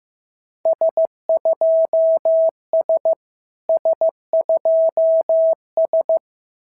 Morse code, heard as a keyed tone, about 14 wpm, 690 Hz. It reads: S2S S2S
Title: SOTA QSO Practice 15wpm